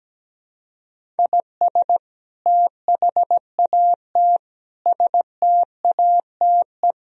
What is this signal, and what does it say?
Morse code, heard as a keyed tone, about 17 wpm, 700 Hz. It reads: IS THAT STATE
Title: Sentences from Top 1000 Words - Review Part 3 - 17wpm